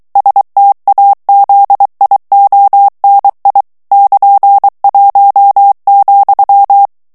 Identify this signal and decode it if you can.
Morse code, heard as a keyed tone, about 23 wpm, 800 Hz. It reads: STAZIONI <KN>1,
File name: ITALIEN texte clair 2 vitesse 22 mots minute